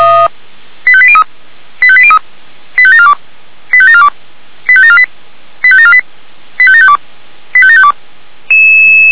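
Five-tone selective call (ZVEI), heard as quick runs of five tones, tones of about 70 ms. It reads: 85803 85803 85742 85742 85748 85748 85752 85752
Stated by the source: ZVEI five-tone call received by SDR.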